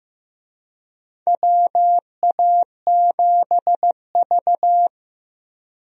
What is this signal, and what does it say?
Morse code, heard as a keyed tone, about 15 wpm, 700 Hz. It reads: WA7V
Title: USA Call Signs - 4 Characters - Repeated 2x - 15wpm